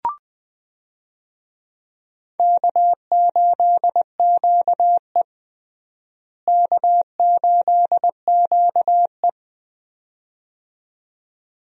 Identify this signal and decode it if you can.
Morse code, heard as a keyed tone, about 21 wpm, 700 Hz. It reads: K8QE K8QE
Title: USA Call Signs - 4 Characters - Repeated 2x - 20wpm